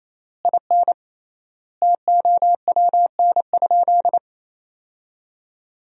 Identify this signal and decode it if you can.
Morse code, heard as a keyed tone, about 28 wpm, 700 Hz. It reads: IN TOWN?
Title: 3x Word-Spacing - Sentences from Top 500 Words - 28wpm